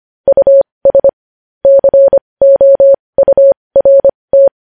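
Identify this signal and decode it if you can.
Morse code, heard as a keyed tone, about 25 wpm, spaced as for 20 wpm, 550 Hz. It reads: US COURT
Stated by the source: Morse code sending news headlines at 20 wpm.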